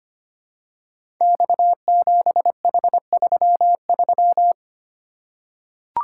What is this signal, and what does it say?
Morse code, heard as a keyed tone, about 25 wpm, 700 Hz. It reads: X7H33
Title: Code Groups - 5 Letter:Numbers - 25wpm